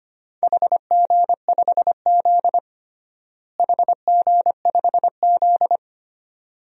Morse code, heard as a keyed tone, about 25 wpm, 700 Hz. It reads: HG5Z HG5Z
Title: DXCC-100 Call Signs - 4 Characters - Repeated 2x - 25wpm